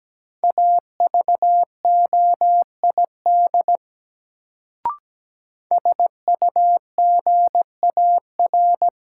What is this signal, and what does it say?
Morse code, heard as a keyed tone, about 17 wpm, 700 Hz. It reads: AVOID SUGAR
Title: Sentences from Top 1200 Words - New Word Drill Alt - 17wpm